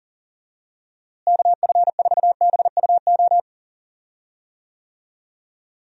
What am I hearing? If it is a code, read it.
Morse code, heard as a keyed tone, about 40 wpm, 700 Hz. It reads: KF4BUO
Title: USA Call Signs - 6 Characters - Repeated 3x - 40wpm